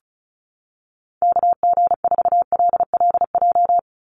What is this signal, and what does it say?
Morse code, heard as a keyed tone, about 34 wpm, 700 Hz. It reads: KG4LLJ
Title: USA Call Signs 35wpm